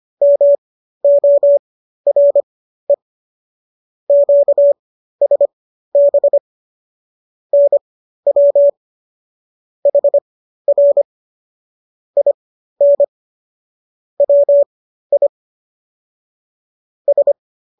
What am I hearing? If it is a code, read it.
Morse code, heard as a keyed tone, about 25 wpm, spaced as for 13 wpm, 570 Hz. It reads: MORE QSB NW HR IN WI S